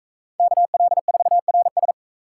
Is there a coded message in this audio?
Yes: Morse code from 0.4 to 1.9 s, heard as a keyed tone, about 41 wpm, 700 Hz. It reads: KL4RS